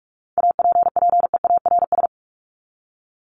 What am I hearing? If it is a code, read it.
Morse code, heard as a keyed tone, about 44 wpm, 700 Hz. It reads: APPEARS
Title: ES:S Suffix - Top 100 Words with Base Words 45wpm